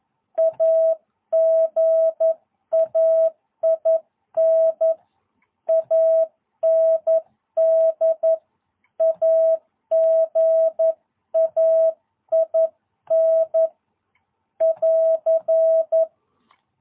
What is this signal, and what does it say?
Morse code, heard as a keyed tone, about 11 wpm, 660 Hz. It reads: AGAIN AND AGAIN <AR>
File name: jh8cbh / 2022cw